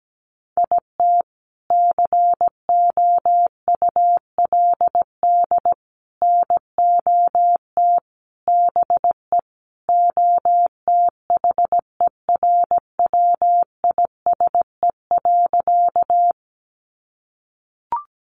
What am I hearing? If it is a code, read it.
Morse code, heard as a keyed tone, about 17 wpm, 700 Hz. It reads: IT COULD NOT BE OTHERWISE.
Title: Sentences from Top 1200 Words 17wpm